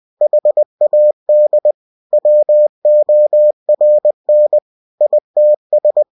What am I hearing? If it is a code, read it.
Morse code, heard as a keyed tone, about 20 wpm, 600 Hz. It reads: HAD WORN ITS
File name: cw-books / winnie-the-pooh / winnie-20wpm-page-0028